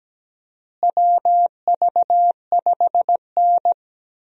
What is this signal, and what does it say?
Morse code, heard as a keyed tone, about 17 wpm, 700 Hz. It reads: WV5N
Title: USA Call Signs - 4 Characters - Repeated 3x - 17wpm